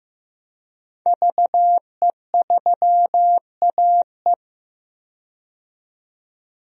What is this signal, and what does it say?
Morse code, heard as a keyed tone, about 15 wpm, 700 Hz. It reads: VE3AE
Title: DXCC-100 Call Signs - 5 Characters - 15wpm